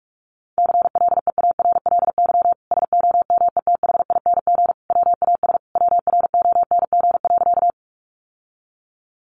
Morse code, heard as a keyed tone, about 44 wpm, 700 Hz. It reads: CLEARLY SOMETHING WAS WRONG.
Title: Sentences from Top 1300 Words 45wpm